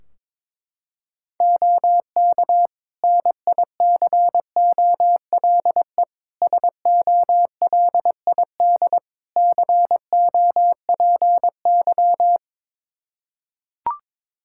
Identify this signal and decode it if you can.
Morse code, heard as a keyed tone, about 22 wpm, 700 Hz. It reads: OK NICOLE SOLID COPY
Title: Practice QSO - Exchange 3 - More Info - 22wpm